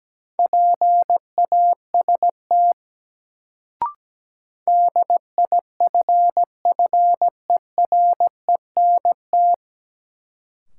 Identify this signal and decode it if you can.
Morse code, heard as a keyed tone, about 17 wpm, 700 Hz. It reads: PAST DIFFERENT
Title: Sentences from Top 600 Words - New Word Drill - 17wpm